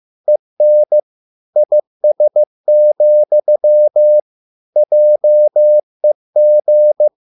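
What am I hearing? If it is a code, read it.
Morse code, heard as a keyed tone, about 15 wpm, 600 Hz. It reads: EN IS, JEG
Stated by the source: Morse code at a slow 15 wpm from CW med Gnister.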